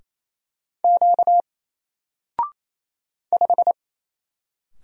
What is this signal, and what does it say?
Morse code, heard as a keyed tone, about 28 wpm, 700 Hz. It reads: Q 5
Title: Single Character (Letter:Number:Punctuation:Prosign) Alt 28wpm